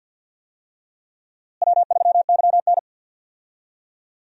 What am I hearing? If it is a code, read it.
Morse code, heard as a keyed tone, about 52 wpm, 700 Hz. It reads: W3YN
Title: USA Call Signs - 4 Characters - Repeated 3x - 50wpm